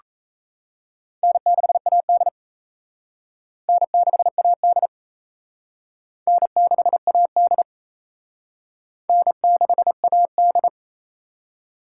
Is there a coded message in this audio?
Yes: Morse code from 1.2 to 10.7 s, heard as a keyed tone, about 35 wpm, 700 Hz. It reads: N6AD N6AD N6AD N6AD